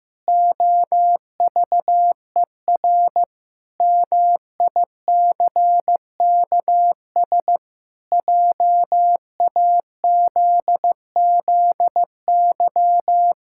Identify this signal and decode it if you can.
Morse code, heard as a keyed tone, about 15 wpm, 700 Hz. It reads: OVER MICKS JAZZY
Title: Pangram Challenge v2 - 15wpm